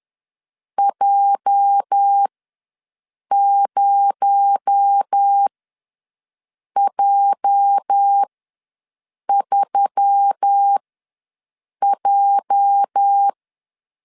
Morse code, heard as a keyed tone, about 10 wpm, spaced as for 6 wpm, 800 Hz. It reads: J0J3J
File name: LEZIONI / 8 YJ45 / 8 YJ ws 10wpm 4043